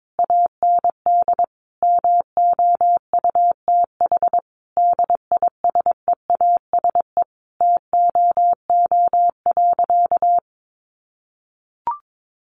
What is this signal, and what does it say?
Morse code, heard as a keyed tone, about 22 wpm, 700 Hz. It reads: AND MOUTH DISEASE TOO.
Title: Sentences from Top 1500 Words 22wpm